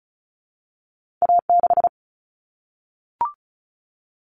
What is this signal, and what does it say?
Morse code, heard as a keyed tone, about 36 wpm, 700 Hz. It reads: A6
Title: Common Cut Numbers - 2-Digit 35wpm